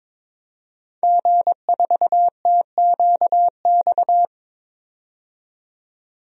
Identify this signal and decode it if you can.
Morse code, heard as a keyed tone, about 22 wpm, 700 Hz. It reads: G4TQX
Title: DXCC-100 Call Signs - 5 Characters - 22wpm